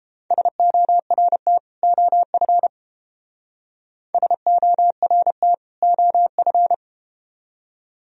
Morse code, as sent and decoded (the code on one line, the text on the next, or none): ... --- .-. - / --- ..-. / ... --- .-. - / --- ..-.
SORT OF SORT OF